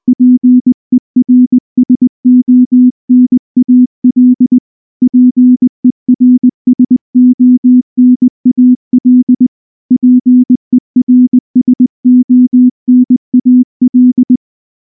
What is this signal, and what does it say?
Morse code, heard as a keyed tone, about 20 wpm, 250 Hz. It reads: PERSONAL PERSONAL PERSONAL
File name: morse / morse-training-20WPM-5